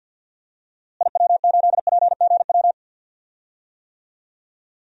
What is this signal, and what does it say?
Morse code, heard as a keyed tone, about 49 wpm, 700 Hz. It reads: IW8PGW
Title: DXCC-100 Call Signs - 6 Characters - Repeated 2x - 50wpm